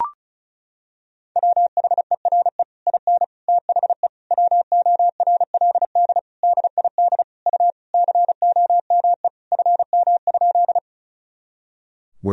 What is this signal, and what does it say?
Morse code, heard as a keyed tone, about 34 wpm, 700 Hz. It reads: WHERE IN THE WORLD DID U COME FM?
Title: Sentences from Top 200 Words and Abbreviations 35wpm